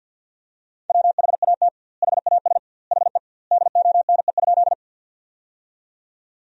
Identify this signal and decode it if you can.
Morse code, heard as a keyed tone, about 51 wpm, 700 Hz. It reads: WHAT HAS HE DONE?
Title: 2x Word-Spacing - Sentences from Top 500 Words - 50wpm